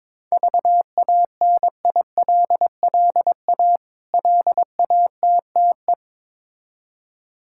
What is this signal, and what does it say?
Morse code, heard as a keyed tone, about 22 wpm, 700 Hz. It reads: VANILLA LATTE
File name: Difficult Words with Hints - 22wpm